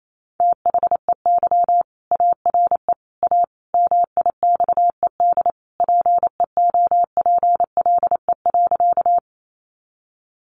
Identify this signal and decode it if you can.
Morse code, heard as a keyed tone, about 28 wpm, 700 Hz. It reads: THEY ARE A MIXED PEOPLE.
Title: Sentences from Top 1500 Words 28wpm